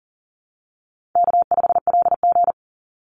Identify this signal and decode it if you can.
Morse code, heard as a keyed tone, about 40 wpm, 700 Hz. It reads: K5LG
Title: Call Signs Repeated 3x - 40wpm